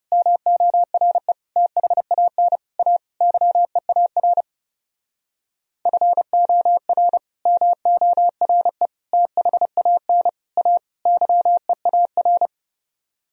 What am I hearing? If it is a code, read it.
Morse code, heard as a keyed tone, about 37 wpm, 700 Hz. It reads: MORE THAN A YEAR FOR MORE THAN A YEAR
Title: Speed-Racing - Sets of 5 Words 30wpm